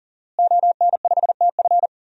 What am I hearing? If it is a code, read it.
Morse code, heard as a keyed tone, about 40 wpm, 700 Hz. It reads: ON5TF
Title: DXCC-100 Call Signs - 5 Characters - Repeated 2x - 40wpm